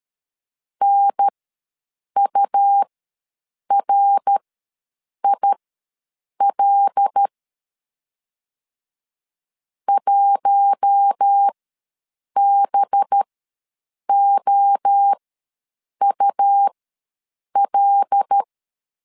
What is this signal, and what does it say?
Morse code, heard as a keyed tone, about 13 wpm, spaced as for 7 wpm, 800 Hz. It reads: NURIL 1BOUL
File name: 5 LB ws 12wpm